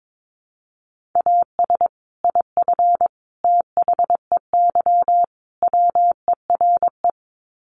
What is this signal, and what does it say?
Morse code, heard as a keyed tone, about 22 wpm, 700 Hz. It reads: AS IF THEY WERE
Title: Sets of 4 Words 22wpm